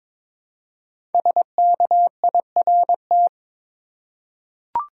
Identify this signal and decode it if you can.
Morse code, heard as a keyed tone, about 22 wpm, 700 Hz. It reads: SKIRT
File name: 5 Letter Words - First 3 Letters Given - 22wpm